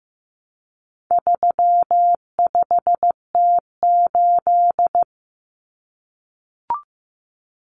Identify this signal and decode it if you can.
Morse code, heard as a keyed tone, about 15 wpm, 700 Hz. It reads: 35T8
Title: Common Cut Numbers - 4-Digit 15wpm